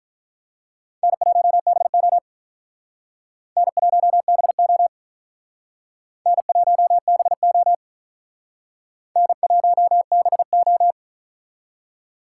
Morse code, heard as a keyed tone, about 43 wpm, 700 Hz. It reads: N1BO N1BO N1BO N1BO
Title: Speed Racing - USA 1x2 Call Signs 35wpm